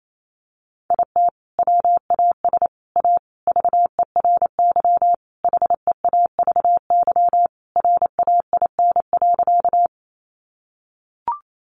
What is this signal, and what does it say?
Morse code, heard as a keyed tone, about 28 wpm, 700 Hz. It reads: IT WAS A VERY HEAVY RAIN.
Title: Sentences from Top 1100 Words 28wpm